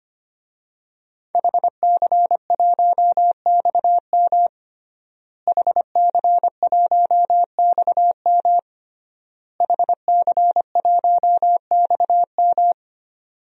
Morse code, heard as a keyed tone, about 25 wpm, 700 Hz. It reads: HC1XM HC1XM HC1XM
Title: DXCC-100 Call Signs - 5 Characters - Repeated 3x - 25wpm